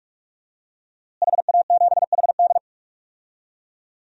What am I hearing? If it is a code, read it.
Morse code, heard as a keyed tone, about 46 wpm, 700 Hz. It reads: HA7HD
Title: DXCC-100 Call Signs - 5 Characters - Repeated 3x - 45wpm